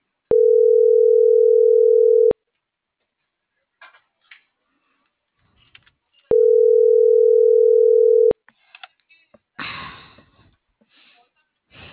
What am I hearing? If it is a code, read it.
Ringback tone (North America), 440 + 480 Hz.